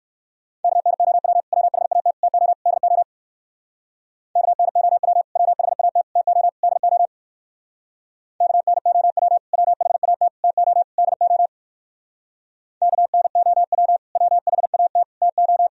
Morse code, heard as a keyed tone, about 56 wpm, 700 Hz. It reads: KNOW WHAT TO DO KNOW WHAT TO DO KNOW WHAT TO DO KNOW WHAT TO